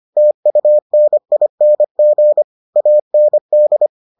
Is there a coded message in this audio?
Yes: Morse code from 0.2 to 3.9 s, heard as a keyed tone, about 25 wpm, 600 Hz. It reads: TUNING AND